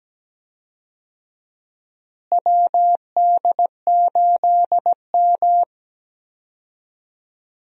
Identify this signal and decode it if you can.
Morse code, heard as a keyed tone, about 17 wpm, 700 Hz. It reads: WD8M
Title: USA Call Signs - 4 Characters - Repeated 3x - 17wpm